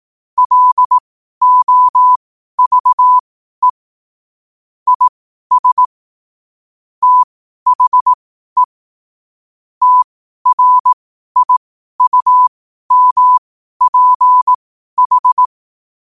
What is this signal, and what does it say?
Morse code, heard as a keyed tone, about 18 wpm, spaced as for 12 wpm, 1000 Hz. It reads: LOVE IS THE TRIUMPH
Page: Quotes for Fri, 15 Aug 2025 in Morse Code at 12 words per minute.